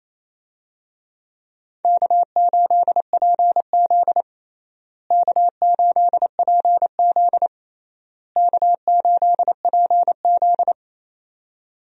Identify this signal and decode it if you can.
Morse code, heard as a keyed tone, about 27 wpm, 700 Hz. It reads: K8PZ K8PZ K8PZ
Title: USA Call Signs - 4 Characters - Repeated 3x - 28wpm